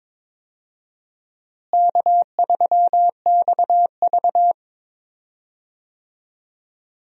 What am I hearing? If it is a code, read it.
Morse code, heard as a keyed tone, about 22 wpm, 700 Hz. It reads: K3XV
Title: USA Call Signs - 4 Characters - Repeated 2x - 22wpm